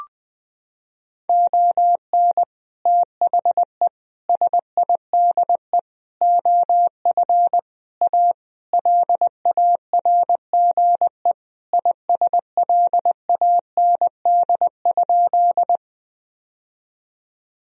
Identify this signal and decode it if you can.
Morse code, heard as a keyed tone, about 20 wpm, 700 Hz. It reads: ON THE SIDE OF A LARGE ISLAND?
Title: Sentences from Top 1000 Words - Review Part 3 - 20wpm